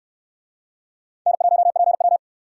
Morse code, heard as a keyed tone, about 68 wpm, 700 Hz. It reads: N1PW